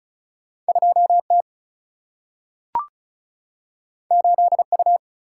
Morse code, heard as a keyed tone, about 36 wpm, 700 Hz. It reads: 2T 8U
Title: Single-Letter Single-Number 35wpm